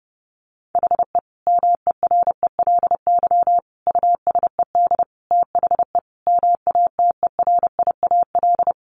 Morse code, heard as a keyed tone, about 30 wpm, 700 Hz. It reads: HE MERELY USED THE MATERIAL
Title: Sentences from Top 1400 Words 30wpm